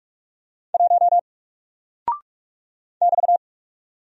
Morse code, heard as a keyed tone, about 44 wpm, 700 Hz. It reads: 1 <BT>
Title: Single Character (Letter:Number:Punctuation:Prosign) Alt 45wpm